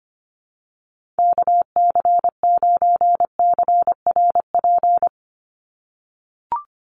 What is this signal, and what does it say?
Morse code, heard as a keyed tone, about 25 wpm, 690 Hz. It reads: KC9CRP
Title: USA Call Signs 25wpm